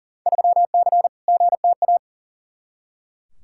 Morse code, heard as a keyed tone, about 40 wpm, 700 Hz. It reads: 3C GTA
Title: Field Day Exchanges Encore - 40wpm